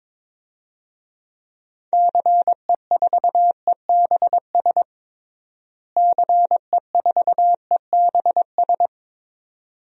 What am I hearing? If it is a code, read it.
Morse code, heard as a keyed tone, about 22 wpm, 700 Hz. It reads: CE4EBS CE4EBS
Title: DXCC-100 Call Signs - 6 Characters - Repeated 2x - 22wpm